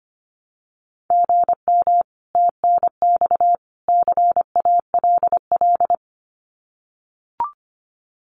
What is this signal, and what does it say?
Morse code, heard as a keyed tone, about 25 wpm, 700 Hz. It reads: GM TNX CALL
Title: Common QSO Phrases 25wpm